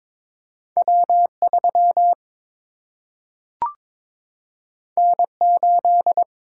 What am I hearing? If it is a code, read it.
Morse code, heard as a keyed tone, about 22 wpm, 700 Hz. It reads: W 3 N 8
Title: Single-Letter Single-Number 22wpm